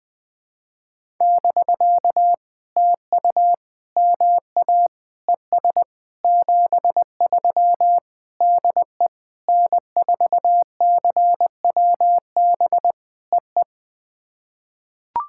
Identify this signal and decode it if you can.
Morse code, heard as a keyed tone, about 20 wpm, 700 Hz. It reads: <BK> TU MA ES 73 DE N4CWB EE
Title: POTA QSO Practice - 20wpm